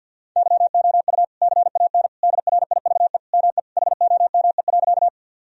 Morse code, heard as a keyed tone, about 46 wpm, 700 Hz. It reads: YOU CAN DRIVE ME HOME.
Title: Sentences from Top 1000 Words - Review Part 2 - 50wpm